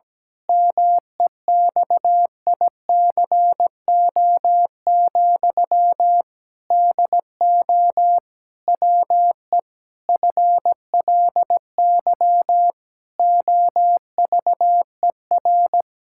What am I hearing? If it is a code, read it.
Morse code, heard as a keyed tone, about 17 wpm, 700 Hz. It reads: MEXICO, DO WE FLY OVER
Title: Pangram Challenge v2 - 17wpm